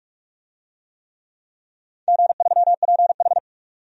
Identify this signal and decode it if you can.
Morse code, heard as a keyed tone, about 46 wpm, 700 Hz. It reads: G3PH
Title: DXCC-100 Call Signs - 4 Characters - 45wpm